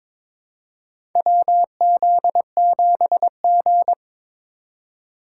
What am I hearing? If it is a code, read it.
Morse code, heard as a keyed tone, about 22 wpm, 700 Hz. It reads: WZ7G
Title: USA Call Signs - 4 Characters - 2x Character Spacing - 22wpm